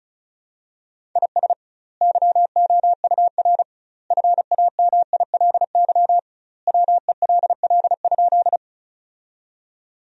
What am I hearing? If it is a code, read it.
Morse code, heard as a keyed tone, about 36 wpm, 700 Hz. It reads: IS YOUR FAMILY WELL?
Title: 2x Word-Spacing - Sentences from Top 400 Words - 35wpm